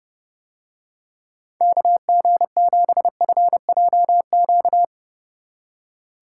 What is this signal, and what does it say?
Morse code, heard as a keyed tone, about 29 wpm, 700 Hz. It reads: KG7FJQ